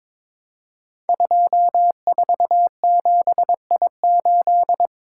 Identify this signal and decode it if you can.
Morse code, heard as a keyed tone, about 22 wpm, 700 Hz. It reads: 247I8
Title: Code Groups - 5 Letter:Numbers - 22wpm